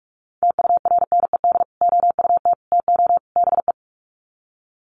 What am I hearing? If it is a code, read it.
Morse code, heard as a keyed tone, about 44 wpm, 700 Hz. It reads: TURNED OUT TO BE
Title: Sets of 4 Words 45wpm